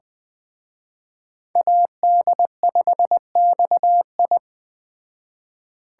Morse code, heard as a keyed tone, about 20 wpm, 700 Hz. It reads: AD5XI
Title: Speed Racing - USA 2x2 Call Signs 17wpm